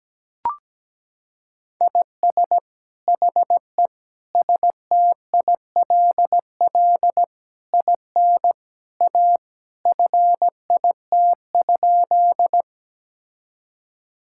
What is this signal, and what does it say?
Morse code, heard as a keyed tone, about 17 wpm, 700 Hz. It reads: IS HE STILL IN A FIT?
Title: Sentences from Top 1000 Words - Review Part 5 - 17wpm